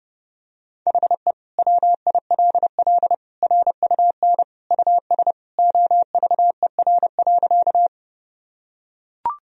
Morse code, heard as a keyed tone, about 32 wpm, 700 Hz. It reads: HE WILL RUN US OVER.
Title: Sentences from Top 1000 Words - Review Part 1 - 30wpm